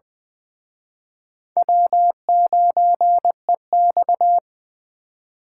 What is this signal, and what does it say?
Morse code, heard as a keyed tone, about 24 wpm, 700 Hz. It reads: W9EX
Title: Speed Racing - USA Call Signs 20wpm